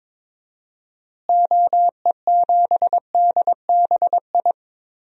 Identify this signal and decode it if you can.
Morse code, heard as a keyed tone, about 22 wpm, 700 Hz. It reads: OE7DBI
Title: DXCC-100 Call Signs - 6 Characters - Repeated 3x - 22wpm